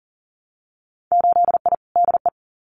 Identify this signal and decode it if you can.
Morse code, heard as a keyed tone, about 40 wpm, 700 Hz. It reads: 8I DE
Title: Winter Field Day Exchanges Encore 40wpm